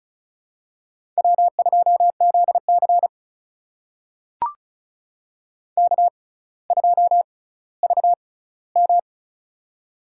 Morse code, heard as a keyed tone, about 36 wpm, spaced as for 18 wpm, 700 Hz. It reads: W2ZC K2VM
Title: USA Call Signs - 4 Characters - 6x Character Spacing - 35wpm